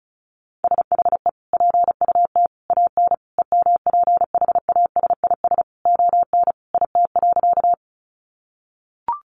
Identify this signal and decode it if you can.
Morse code, heard as a keyed tone, about 34 wpm, 700 Hz. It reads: SHE PUT AN EMPHASIS ON IT.
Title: Sentences from Top 1400 Words 35wpm